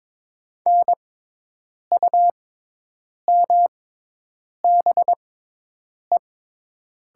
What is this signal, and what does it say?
Morse code, heard as a keyed tone, about 22 wpm, spaced as for 8 wpm, 700 Hz. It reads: NUMBE
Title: 6x Character Spacing - Top 100 Words - 22wpm